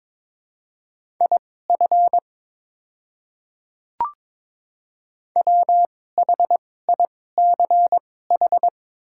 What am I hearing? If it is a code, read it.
Morse code, heard as a keyed tone, about 22 wpm, spaced as for 16 wpm, 700 Hz. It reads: IF WHICH
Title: 2x Character Spacing - Top 100 Words - 22wpm